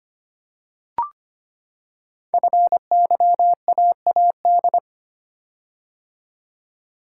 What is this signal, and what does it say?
Morse code, heard as a keyed tone, about 25 wpm, 700 Hz. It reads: FYAAD
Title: Code Groups - 5 Letters - 25wpm